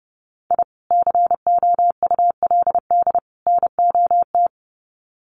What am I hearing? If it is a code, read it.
Morse code, heard as a keyed tone, about 30 wpm, 700 Hz. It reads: I COULD NOT
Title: Sets of 3 Words 30wpm